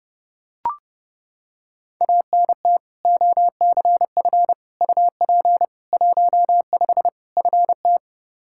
Morse code, heard as a keyed tone, about 30 wpm, 700 Hz. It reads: ANT OCF UP 15 FT
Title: Practice QSO - Exchange 2 - Info - 30wpm